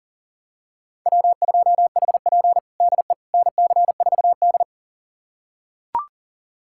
Morse code, heard as a keyed tone, about 40 wpm, 700 Hz. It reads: W2HP DE NC4D
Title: USA Callsign DE Callsign - 4 Characters - 40wpm